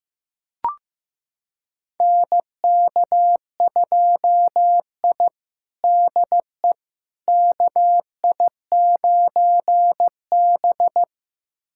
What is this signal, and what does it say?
Morse code, heard as a keyed tone, about 15 wpm, 700 Hz. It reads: NK2I DE KI9B
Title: USA Callsign DE Callsign - 4 Characters - 15wpm